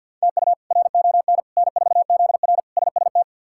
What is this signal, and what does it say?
Morse code, heard as a keyed tone, about 52 wpm, 700 Hz. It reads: TU RON N4ZR SST
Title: K1USN SST QSO Practice - 50wpm